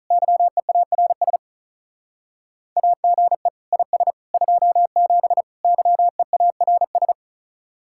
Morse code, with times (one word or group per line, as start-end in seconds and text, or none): 0.1-1.4 YEARS
2.8-3.5 AGE
3.7-4.1 IS
4.3-5.4 27
5.6-7.1 YEARS